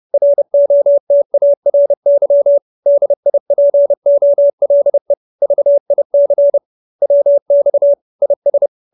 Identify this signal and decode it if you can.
Morse code, heard as a keyed tone, about 30 wpm, 570 Hz. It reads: ROTARY DIPOLE VIC WX IS